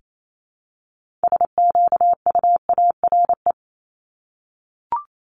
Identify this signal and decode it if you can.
Morse code, heard as a keyed tone, about 28 wpm, 700 Hz. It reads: SQUARE
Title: Multi-letter Phonemes 28wpm